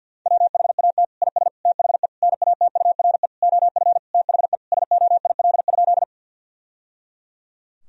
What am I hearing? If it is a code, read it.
Morse code, heard as a keyed tone, about 50 wpm, 700 Hz. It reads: WHAT IS THE NATURE OF THE SOIL?
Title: Sentences from Top 1000 Words - Review Part 4 - 50wpm